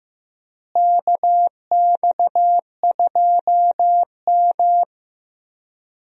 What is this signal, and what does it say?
Morse code, heard as a keyed tone, about 15 wpm, 700 Hz. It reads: KX2M
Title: USA Call Signs - 4 Characters - Repeated 2x - 15wpm